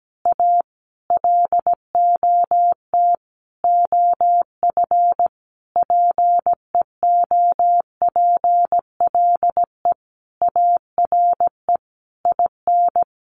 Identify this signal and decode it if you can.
Morse code, heard as a keyed tone, about 17 wpm, 700 Hz. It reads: A LOT OF PEOPLE ARE IN
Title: Additional Easy Phrases 17wpm